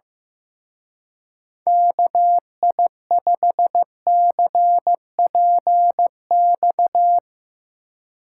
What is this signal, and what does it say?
Morse code, heard as a keyed tone, about 16 wpm, 700 Hz. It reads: KI5CPX
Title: Speed Racing - USA Call Signs 15wpm